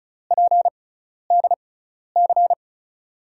Morse code, heard as a keyed tone, about 36 wpm, spaced as for 12 wpm, 700 Hz. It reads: PDC
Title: USA Call Signs - 6x Character Spacing - 35wpm